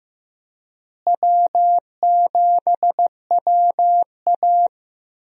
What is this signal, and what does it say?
Morse code, heard as a keyed tone, about 16 wpm, 700 Hz. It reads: W7WA
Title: USA Call Signs - 4 Characters - Repeated 3x - 15wpm